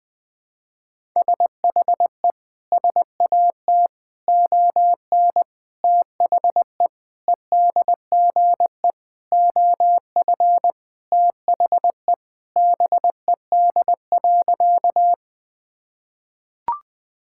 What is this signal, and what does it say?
Morse code, heard as a keyed tone, about 20 wpm, 700 Hz. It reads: SHE SAT ON THE EDGE OF THE BED.
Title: Sentences with 4-Word Phrases - Spoken Beginning - 20wpm